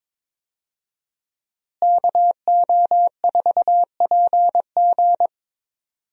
Morse code, heard as a keyed tone, about 22 wpm, 700 Hz. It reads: KO4PG
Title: USA Call Signs - 5 Characters - Repeated 3x - 22wpm